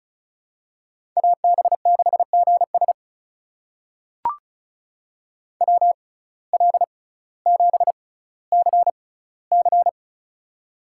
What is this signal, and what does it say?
Morse code, heard as a keyed tone, about 36 wpm, spaced as for 18 wpm, 700 Hz. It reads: AB6GS WL7CC
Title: USA Call Signs - 6x Character Spacing - 35wpm